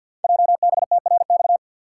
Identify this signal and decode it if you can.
Morse code, heard as a keyed tone, about 50 wpm, 700 Hz. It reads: JBTRX
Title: Code Groups - 5 Letters - 50wpm